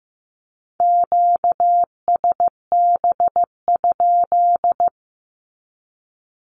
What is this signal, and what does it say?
Morse code, heard as a keyed tone, about 15 wpm, 700 Hz. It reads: QSB?
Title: Q-Codes with Spoken Meaning - 15wpm